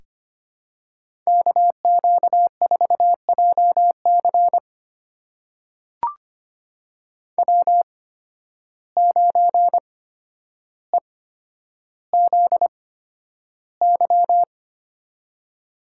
Morse code, heard as a keyed tone, about 25 wpm, spaced as for 11 wpm, 700 Hz. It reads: KQ4JC W9EZY
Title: USA Call Signs - 5 Characters - 8x Character Spacing - 25wpm